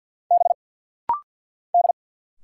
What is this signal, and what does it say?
Morse code, heard as a keyed tone, about 49 wpm, 700 Hz. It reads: B D
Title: Single Letter-Number - Rapid-Fire 50wpm